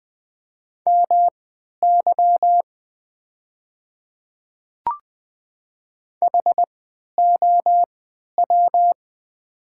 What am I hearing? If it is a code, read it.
Morse code, heard as a keyed tone, about 20 wpm, spaced as for 11 wpm, 700 Hz. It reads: MY HOW
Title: 3x Character Spacing - Top 100 Words - 20wpm